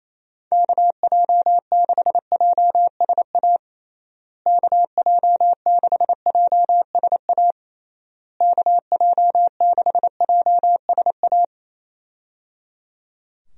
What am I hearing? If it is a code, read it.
Morse code, heard as a keyed tone, about 27 wpm, 700 Hz. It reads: KJ6JSA KJ6JSA KJ6JSA
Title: USA Call Signs - 6 Characters - Repeated 3x - 28wpm